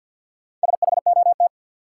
Morse code, heard as a keyed tone, about 46 wpm, 700 Hz. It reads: SHOT
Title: Sentences from Top 1100 Words - New Word Drill Alt - 50wpm